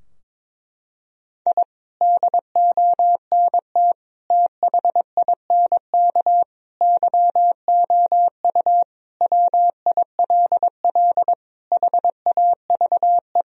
Morse code, heard as a keyed tone, about 22 wpm, 700 Hz. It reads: I DONT THINK YOU WILL HAVE
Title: Sentences from Top 1000 Words - Review Part 4 - 22wpm